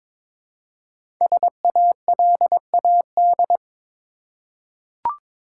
Morse code, heard as a keyed tone, about 22 wpm, 700 Hz. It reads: SALAD
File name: 5 Letter Words - First 3 Letters Given - 22wpm